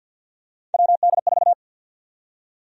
Morse code, heard as a keyed tone, about 50 wpm, 700 Hz. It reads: WD4
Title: USA Call Sign Prefixes - 50wpm